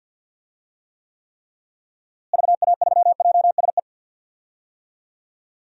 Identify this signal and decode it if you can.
Morse code, heard as a keyed tone, about 52 wpm, 700 Hz. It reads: VA3JSE
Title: DXCC-100 Call Signs - 6 Characters - Repeated 3x - 50wpm